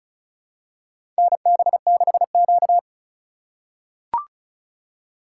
Morse code, heard as a keyed tone, about 34 wpm, 700 Hz. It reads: NB6Q
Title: USA Call Signs - 4 Characters - 3x Character Spacing - 35wpm